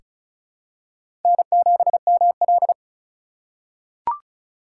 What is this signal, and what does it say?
Morse code, heard as a keyed tone, about 36 wpm, 700 Hz. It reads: N7ML
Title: K1USN SST QSO Practice - 35wpm